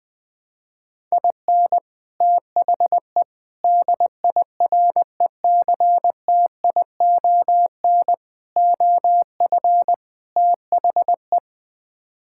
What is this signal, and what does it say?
Morse code, heard as a keyed tone, about 20 wpm, 700 Hz. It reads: IN THE DIRECTION OF THE
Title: Build A Phrase - 5-Word Phrases - 20wpm